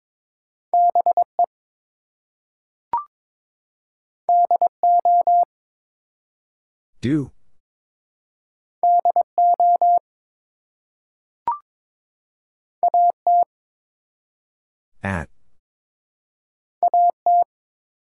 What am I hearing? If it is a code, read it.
Morse code, heard as a keyed tone, about 22 wpm, 700 Hz. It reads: BE DO DO AT AT
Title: Two Letter Words - 22wpm